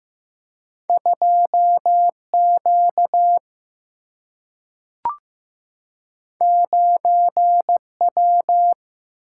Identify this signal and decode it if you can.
Morse code, heard as a keyed tone, about 15 wpm, 700 Hz. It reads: 2Q 9W
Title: Single-Letter Single-Number 15wpm